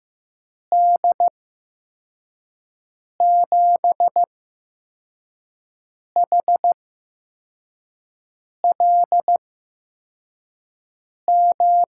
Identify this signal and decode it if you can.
Morse code, heard as a keyed tone, about 15 wpm, spaced as for 4 wpm, 700 Hz. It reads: D7HLM
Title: USA Call Signs - 6 Characters - 8x Character Spacing - 15wpm